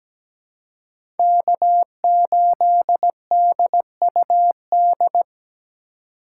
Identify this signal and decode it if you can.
Morse code, heard as a keyed tone, about 17 wpm, 700 Hz. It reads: K8DUD
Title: USA Call Signs - 5 Characters - 3x Character Spacing - 17wpm